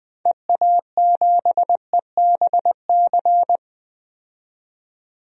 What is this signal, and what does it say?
Morse code, heard as a keyed tone, about 20 wpm, 700 Hz. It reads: EA7EBC
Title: DXCC-100 Call Signs - 6 Characters - 20wpm